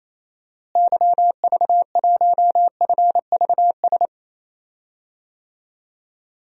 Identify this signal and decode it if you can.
Morse code, heard as a keyed tone, about 28 wpm, 700 Hz. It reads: YV1FVS
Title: DXCC-100 Call Signs - 6 Characters - Repeated 2x - 28wpm